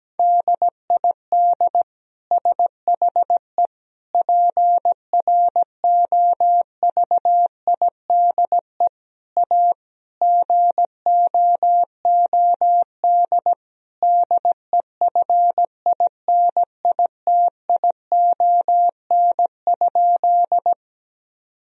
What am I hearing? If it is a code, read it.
Morse code, heard as a keyed tone, about 17 wpm, 700 Hz. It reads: DID SHE PROVIDE A GOOD DEFINITION?
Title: Sentences from Top 1000 Words - Review Part 4 - 17wpm